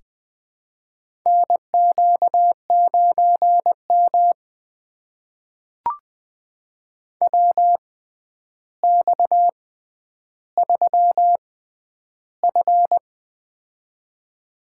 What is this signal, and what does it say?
Morse code, heard as a keyed tone, about 20 wpm, spaced as for 10 wpm, 700 Hz. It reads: NQ9M WX3F
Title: USA Call Signs - 4 Characters - 6x Character Spacing - 20wpm